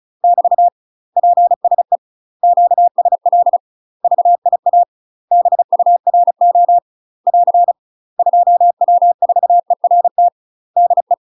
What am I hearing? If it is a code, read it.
Morse code, heard as a keyed tone, about 36 wpm, 700 Hz. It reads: <BT> PSE QSL VIA BURO <AR> 2W4ERT DE
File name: UK MASTER QSOs TRAINING / 35WPM / UK MASTER QSOs 0038